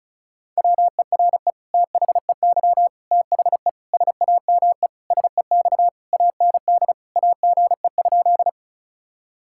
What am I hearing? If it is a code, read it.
Morse code, heard as a keyed tone, about 34 wpm, 700 Hz. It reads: WERE THEY THE SAME SEX AND AGE?
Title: Sentences from Top 1600 Words 35wpm